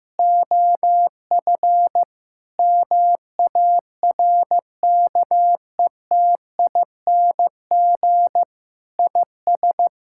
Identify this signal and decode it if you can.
Morse code, heard as a keyed tone, about 15 wpm, 700 Hz. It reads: OF MARKETING IS
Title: Sentences from Top 1000 Words - Review Part 5 - 15wpm